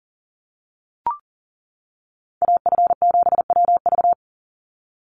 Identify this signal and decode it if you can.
Morse code, heard as a keyed tone, about 40 wpm, 690 Hz. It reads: AF7WV
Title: USA Call Signs 40wpm